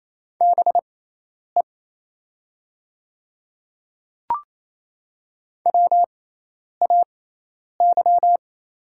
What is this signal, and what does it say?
Morse code, heard as a keyed tone, about 28 wpm, spaced as for 10 wpm, 700 Hz. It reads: BE WAY
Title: 6x Character Spacing - Top 200 Words - 28wpm